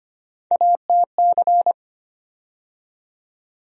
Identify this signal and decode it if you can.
Morse code, heard as a keyed tone, about 25 wpm, 700 Hz. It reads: ATC
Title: USA Call Sign Postfixes - 3 Letters 25wpm